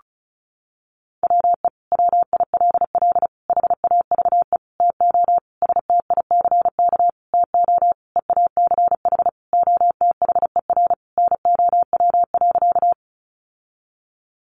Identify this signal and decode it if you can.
Morse code, heard as a keyed tone, about 34 wpm, 700 Hz. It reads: WE WILL HAVE TO STICK TO EACH OTHER NOW.
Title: Sentences from Top 1200 Words 35wpm